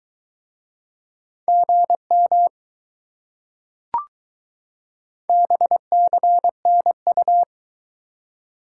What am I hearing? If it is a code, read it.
Morse code, heard as a keyed tone, about 21 wpm, 700 Hz. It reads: GM BCNU